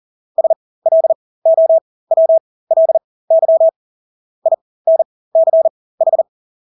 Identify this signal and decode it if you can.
Morse code, heard as a keyed tone, about 40 wpm, spaced as for 20 wpm, 650 Hz. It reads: SLOWLY INCH
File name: Engelske ord 40-20 wpm